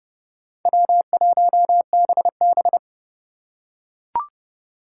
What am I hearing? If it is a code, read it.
Morse code, heard as a keyed tone, about 31 wpm, 700 Hz. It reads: W1BB
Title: USA Call Signs - 4 Characters - 4x Character Spacing - 30wpm